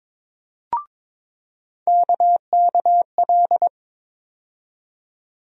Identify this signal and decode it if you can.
Morse code, heard as a keyed tone, about 22 wpm, 700 Hz. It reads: KKL
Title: USA Call Sign Postfixes - 3 Letters 22wpm